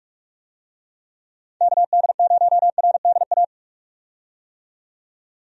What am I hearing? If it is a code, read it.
Morse code, heard as a keyed tone, about 48 wpm, 700 Hz. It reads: KD0RDA
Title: USA Call Signs - 6 Characters - 45wpm